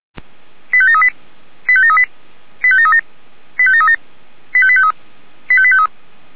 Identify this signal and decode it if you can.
Five-tone selective call (ZVEI), heard as quick runs of five tones, tones of about 70 ms. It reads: 85738 85738 85737 85737 85853 85853